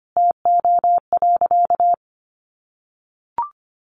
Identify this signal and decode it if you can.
Morse code, heard as a keyed tone, about 25 wpm, 700 Hz. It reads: TO.
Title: Sentences from Top 400 Words 25wpm